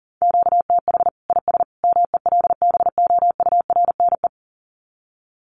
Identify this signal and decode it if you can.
Morse code, heard as a keyed tone, about 40 wpm, 700 Hz. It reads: QTH IS MELBOURNE
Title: Common QSO Phrases 40wpm